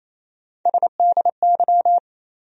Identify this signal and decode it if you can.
Morse code, heard as a keyed tone, about 28 wpm, 700 Hz. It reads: SDY
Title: USA Call Sign Postfixes - 3 Letters 28wpm